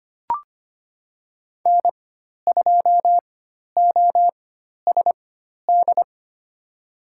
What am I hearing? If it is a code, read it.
Morse code, heard as a keyed tone, about 25 wpm, spaced as for 12 wpm, 700 Hz. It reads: N2OSD
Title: USA Call Signs - 5 Characters - 4x Character Spacing - 25wpm